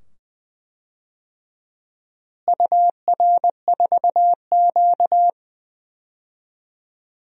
Morse code, heard as a keyed tone, about 20 wpm, 700 Hz. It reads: UR4Q
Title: DXCC-100 Call Signs - 4 Characters - Repeated 2x - 20wpm